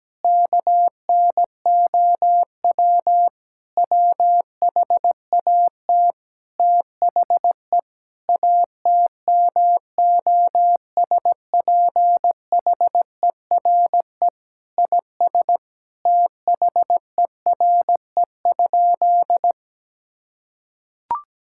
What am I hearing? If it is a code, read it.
Morse code, heard as a keyed tone, about 17 wpm, 700 Hz. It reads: KNOW WHAT THE ATMOSPHERE IS THERE?
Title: Sentences from Top 1600 Words 17wpm